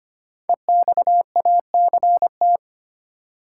Morse code, heard as a keyed tone, about 25 wpm, 700 Hz. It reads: EXACT
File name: 5 Letter Words - First 3 Letters Given - 25wpm